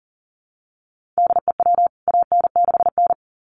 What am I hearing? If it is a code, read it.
Morse code, heard as a keyed tone, about 40 wpm, 700 Hz. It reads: DEW AN6N
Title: CWT Exchange 40wpm